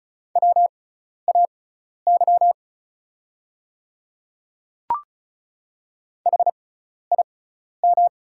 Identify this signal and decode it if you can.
Morse code, heard as a keyed tone, about 36 wpm, spaced as for 12 wpm, 690 Hz. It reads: WAY HIM
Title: 6x Character Spacing - Top 200 Words - 35wpm